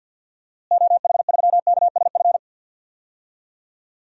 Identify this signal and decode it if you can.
Morse code, heard as a keyed tone, about 47 wpm, 700 Hz. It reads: OH3KSF
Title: DXCC-100 Call Signs - 6 Characters - Repeated 2x - 50wpm